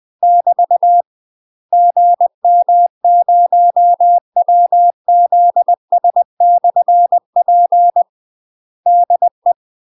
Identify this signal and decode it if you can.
Morse code, heard as a keyed tone, about 20 wpm, 700 Hz. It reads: <BT> GM0WZS/P DE
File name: UK MASTER QSOs TRAINING / 20WPM / UK MASTER QSOs 0068